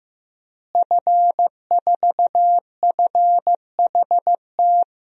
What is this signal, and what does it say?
Morse code, heard as a keyed tone, about 15 wpm, 700 Hz. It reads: F4FHT
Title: DXCC-100 Call Signs - 5 Characters - Repeated 2x - 15wpm